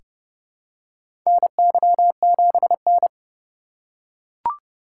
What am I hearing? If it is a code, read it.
Morse code, heard as a keyed tone, about 30 wpm, 700 Hz. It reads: NY7N
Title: K1USN SST QSO Practice - 30wpm